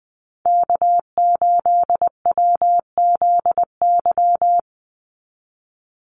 Morse code, heard as a keyed tone, about 20 wpm, 700 Hz. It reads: K8WZY
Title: USA Call Signs 20wpm